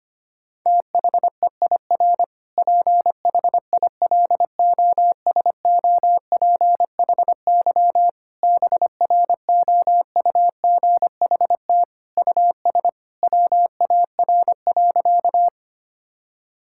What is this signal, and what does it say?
Morse code, heard as a keyed tone, about 25 wpm, 700 Hz. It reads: THEIR PHILOSOPHY BROUGHT US WAR.
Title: Sentences from Top 1000 Words - Review Part 6 - 25wpm